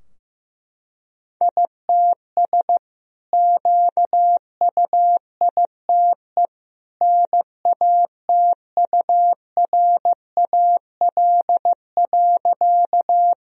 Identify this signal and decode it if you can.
Morse code, heard as a keyed tone, about 15 wpm, 700 Hz. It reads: ITS QUITE NATURAL.
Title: Sentences from Top 1000 Words - Review Part 4 - 15wpm